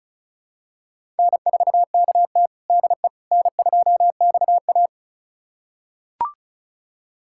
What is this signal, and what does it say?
Morse code, heard as a keyed tone, about 36 wpm, 700 Hz. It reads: N4KT DE N2XA
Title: USA Callsign DE Callsign - 4 Characters - 35wpm